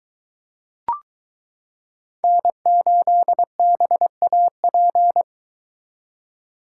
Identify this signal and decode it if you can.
Morse code, heard as a keyed tone, about 22 wpm, 700 Hz. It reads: N8BAP